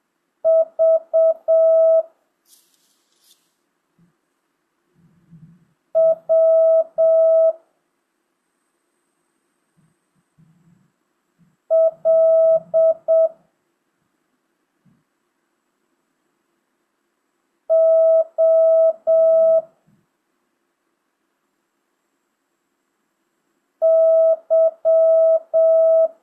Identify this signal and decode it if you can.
Morse code, heard as a keyed tone, about 7 wpm, 650 Hz. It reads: V W L O Y
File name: jh8cbh / 2025cw